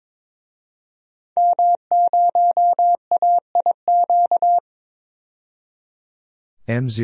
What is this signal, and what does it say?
Morse code, heard as a keyed tone, about 22 wpm, 700 Hz. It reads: M0AIQ
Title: DXCC-100 Call Signs - 5 Characters - 22wpm